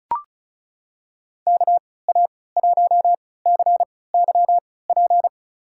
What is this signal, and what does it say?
Morse code, heard as a keyed tone, about 36 wpm, spaced as for 20 wpm, 700 Hz. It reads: KA1CYP
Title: USA Call Signs - 6 Characters - 3x Character Spacing - 35wpm